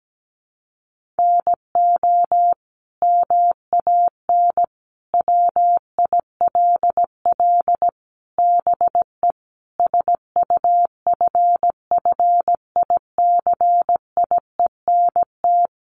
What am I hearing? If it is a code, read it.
Morse code, heard as a keyed tone, about 17 wpm, 700 Hz. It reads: NO MAN WILL BE SUFFICIENT
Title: Sentences from Top 1400 Words 17wpm